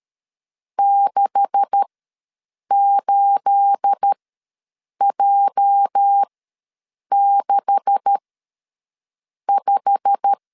Morse code, heard as a keyed tone, about 13 wpm, spaced as for 7 wpm, 800 Hz. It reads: 68J65
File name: LEZIONI / 9 286 ws 12wpm 3351